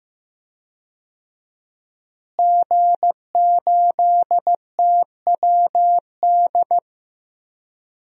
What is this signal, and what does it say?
Morse code, heard as a keyed tone, about 16 wpm, 700 Hz. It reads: G8TWD
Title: DXCC-100 Call Signs - 5 Characters - Repeated 2x - 15wpm